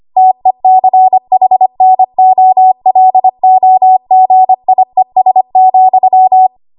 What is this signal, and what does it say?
Morse code, heard as a keyed tone, about 25 wpm, 750 Hz. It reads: TECHNOLOGIES,
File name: Archive / 25 WPM / 25WPM